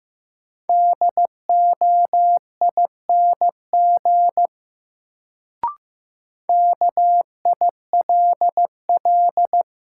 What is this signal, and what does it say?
Morse code, heard as a keyed tone, about 15 wpm, 700 Hz. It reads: DOING KILL
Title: Sentences from Top 1400 Words - New Word Drill Alt - 15wpm